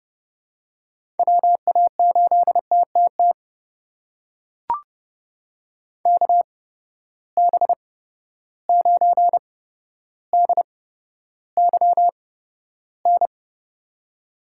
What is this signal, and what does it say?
Morse code, heard as a keyed tone, about 31 wpm, spaced as for 13 wpm, 700 Hz. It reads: WA8TTT KB9DYN
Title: USA Call Signs - 6 Characters - 8x Character Spacing - 30wpm